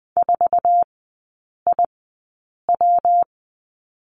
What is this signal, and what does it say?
Morse code, heard as a keyed tone, about 20 wpm, spaced as for 8 wpm, 690 Hz. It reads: 4IW
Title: 133 Lesson 22e - Question Mark - Three Characters 20wpm